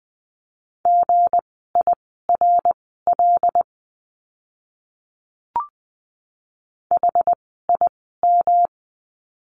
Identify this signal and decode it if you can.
Morse code, heard as a keyed tone, about 20 wpm, spaced as for 14 wpm, 700 Hz. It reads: GIRL HIM
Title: 160 Lesson 26f - Letter g - Words 20wpm